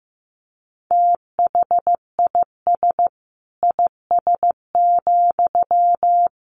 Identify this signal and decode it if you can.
Morse code, heard as a keyed tone, about 15 wpm, 700 Hz. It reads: THIS IS,
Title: Sentences from Top 1500 Words 15wpm